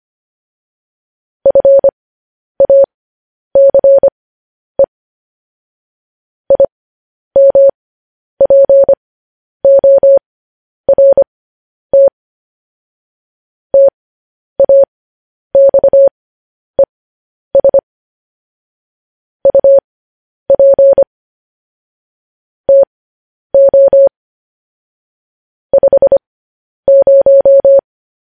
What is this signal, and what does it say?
Morse code, heard as a keyed tone, about 25 wpm, spaced as for 10 wpm, 550 Hz. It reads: FACE IMPORT TAXES UP TO 50
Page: news headlines in morse code at 10 wpm